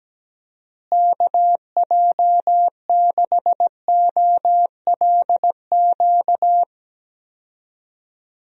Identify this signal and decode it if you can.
Morse code, heard as a keyed tone, about 17 wpm, 700 Hz. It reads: KJ6OLQ
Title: USA Call Signs - 6 Characters - Repeated 2x - 17wpm